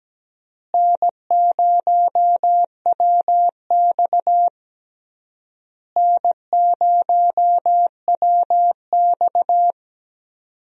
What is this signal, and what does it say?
Morse code, heard as a keyed tone, about 17 wpm, 700 Hz. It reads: N0WX N0WX
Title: USA Call Signs - 4 Characters - Repeated 3x - 17wpm